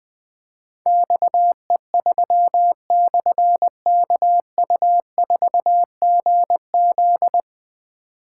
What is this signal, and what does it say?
Morse code, heard as a keyed tone, about 21 wpm, 700 Hz. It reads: XE3/KU4GZ
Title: Difficult Call Signs 20wpm